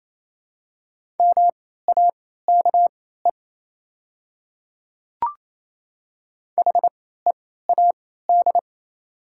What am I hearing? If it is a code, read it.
Morse code, heard as a keyed tone, about 28 wpm, spaced as for 16 wpm, 690 Hz. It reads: MAKE HEAD
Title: 3x Character Spacing - Top 500 Words - 28wpm